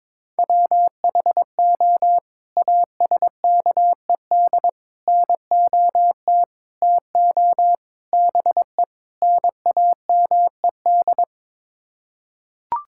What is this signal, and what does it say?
Morse code, heard as a keyed tone, about 22 wpm, 700 Hz. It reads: WHO ASKED NOT TO BE NAMED
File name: practice / 22wpm / Sets of 6 Words - 22wpm